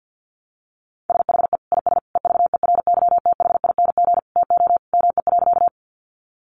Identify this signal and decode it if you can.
Morse code, heard as a keyed tone, about 50 wpm, 700 Hz. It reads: SHE IS EVERYTHING TO ME.
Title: Sentences from Top 700 Words 50wpm